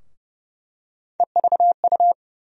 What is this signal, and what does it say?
Morse code, heard as a keyed tone, about 30 wpm, 700 Hz. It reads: EVU
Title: USA Call Sign Postfixes - 3 Letters 30wpm